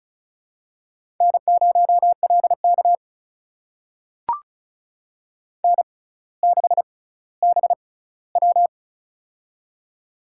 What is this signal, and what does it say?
Morse code, heard as a keyed tone, about 36 wpm, spaced as for 18 wpm, 700 Hz. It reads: N0LK N6BW
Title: USA Call Signs - 4 Characters - 6x Character Spacing - 35wpm